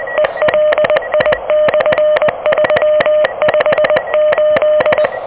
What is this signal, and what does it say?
Morse code, heard as a keyed tone, about 21 wpm, 600 Hz, with noise. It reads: ELI/358
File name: OK QRP 2023 / OK QRP 2023